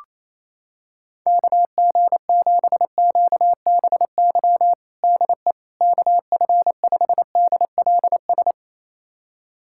Morse code, heard as a keyed tone, about 28 wpm, 700 Hz. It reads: KG7QBY DE KF5DLS
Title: USA Callsign DE Callsign - 6 Characters - 28wpm